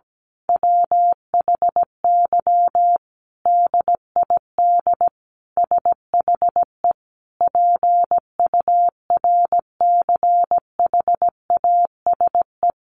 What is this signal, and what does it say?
Morse code, heard as a keyed tone, about 17 wpm, 700 Hz. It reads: WHY DID SHE PURCHASE
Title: Sentences from Top 800 Words 17wpm